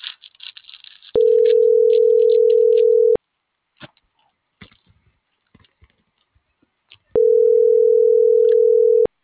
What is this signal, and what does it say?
Ringback tone (North America), 440 + 480 Hz.
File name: CENTRAL TERRESTRE / out cris.f. - .47948